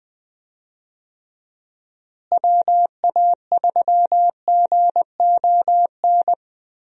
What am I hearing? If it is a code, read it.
Morse code, heard as a keyed tone, about 20 wpm, 700 Hz. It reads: WA3GON
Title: USA Call Signs - 6 Characters - Repeated 2x - 20wpm